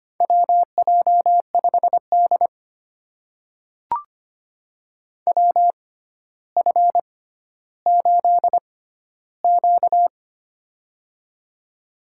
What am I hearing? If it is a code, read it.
Morse code, heard as a keyed tone, about 25 wpm, spaced as for 13 wpm, 700 Hz. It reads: WJ5D WF8Q
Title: USA Call Signs - 4 Characters - 6x Character Spacing - 25wpm